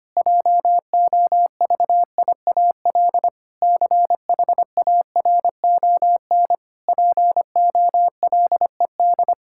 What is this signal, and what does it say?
Morse code, heard as a keyed tone, about 25 wpm, 700 Hz. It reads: JOVIAL CHARON POLED
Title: Pangram Challenge v2 - 25wpm